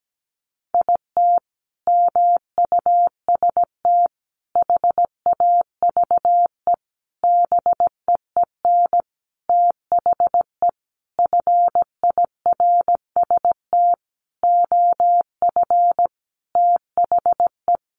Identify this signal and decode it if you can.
Morse code, heard as a keyed tone, about 17 wpm, 700 Hz. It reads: IT MUST HAVE BEEN THE FIRST OF THE
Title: Sentences from Top 800 Words 17wpm